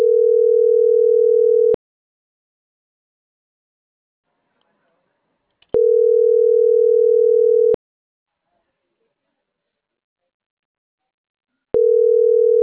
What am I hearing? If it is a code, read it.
Ringback tone (North America), 440 + 480 Hz.